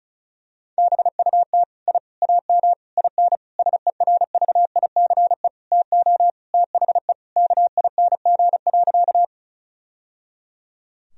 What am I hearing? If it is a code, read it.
Morse code, heard as a keyed tone, about 34 wpm, 700 Hz. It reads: BUT I AM IN SERVICE TO THE KING.
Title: Sentences from Top 1000 Words - Review Part 4 - 35wpm